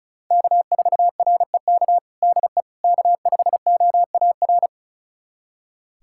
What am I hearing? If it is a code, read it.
Morse code, heard as a keyed tone, about 36 wpm, 700 Hz. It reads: K4REK DE K5OAR
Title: USA Callsign DE Callsign - 5 Characters - 35wpm